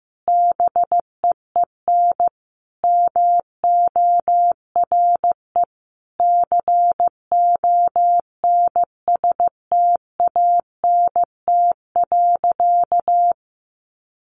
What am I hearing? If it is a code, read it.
Morse code, heard as a keyed tone, about 15 wpm, 700 Hz. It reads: BEEN MORE CONSTANT.
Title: Sentences from Top 1500 Words 15wpm